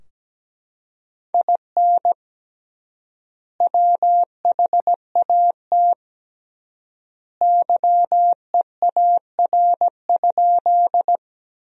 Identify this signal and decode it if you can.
Morse code, heard as a keyed tone, about 17 wpm, 700 Hz. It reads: IN WHAT YEAR?
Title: 3x Word-Spacing - Sentences from Top 200 Words - 17wpm